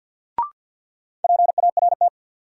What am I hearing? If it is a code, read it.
Morse code, heard as a keyed tone, about 48 wpm, 700 Hz. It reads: PART
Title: Sets of 2 Words - New Word Drill - 50wpm